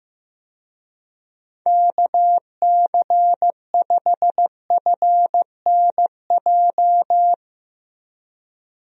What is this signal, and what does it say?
Morse code, heard as a keyed tone, about 15 wpm, 700 Hz. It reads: KC5FNJ
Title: USA Call Signs - 6 Characters - 15wpm